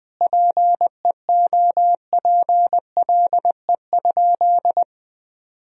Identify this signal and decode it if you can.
Morse code, heard as a keyed tone, about 20 wpm, 700 Hz. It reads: PEOPLE?
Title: Sentences from Top 1000 Words - Review Part 3 - 20wpm